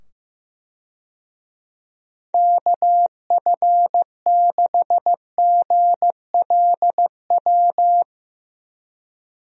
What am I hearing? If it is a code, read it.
Morse code, heard as a keyed tone, about 15 wpm, 700 Hz. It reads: KF6GLW
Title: USA Call Signs - 6 Characters - 15wpm